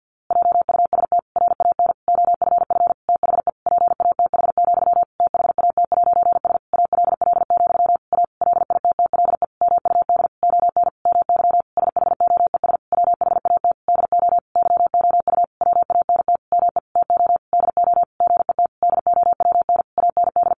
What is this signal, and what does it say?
Morse code, heard as a keyed tone, about 50 wpm, 700 Hz. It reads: JUST RAN OFF THE PATH, THAT'S ALL, A LITTLE MUD ON MY SHOES WHAT DO YOU WANT ME TO DO GET DOWN AND
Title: A Sound Of Thunder 50wpm